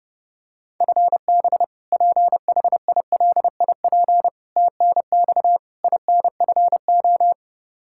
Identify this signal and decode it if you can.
Morse code, heard as a keyed tone, about 32 wpm, 700 Hz. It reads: FB PHILIP TNX INFO
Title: Practice QSO - Exchange 3 - More Info - 30wpm